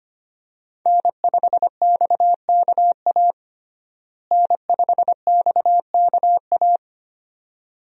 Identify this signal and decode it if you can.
Morse code, heard as a keyed tone, about 25 wpm, 700 Hz. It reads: N5XKA N5XKA
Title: USA Call Signs - 5 Characters - Repeated 2x - 25wpm